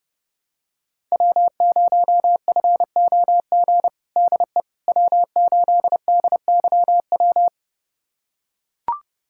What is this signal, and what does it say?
Morse code, heard as a keyed tone, about 32 wpm, 700 Hz. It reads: W0FOG DE W8DYW
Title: USA Callsign DE Callsign - 5 Characters - 30wpm